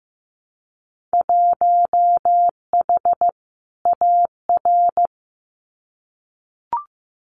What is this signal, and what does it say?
Morse code, heard as a keyed tone, about 15 wpm, 700 Hz. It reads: 1H AR
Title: Winter Field Day Exchanges 15wpm